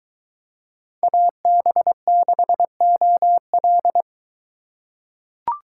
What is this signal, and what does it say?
Morse code, heard as a keyed tone, about 21 wpm, 700 Hz. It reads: AB6OL